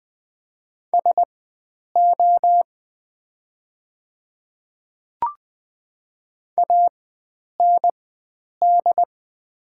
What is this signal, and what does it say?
Morse code, heard as a keyed tone, about 20 wpm, spaced as for 9 wpm, 700 Hz. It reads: SO AND
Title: 4x Character Spacing - Top 100 Words - 20wpm